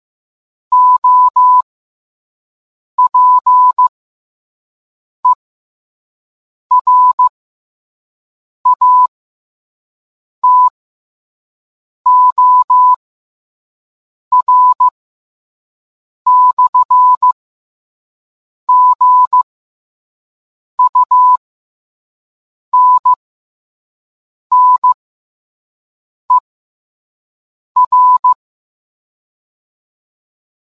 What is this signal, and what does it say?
Morse code, heard as a keyed tone, about 15 wpm, spaced as for 5 wpm, 1000 Hz. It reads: OPERATOR/GUNNER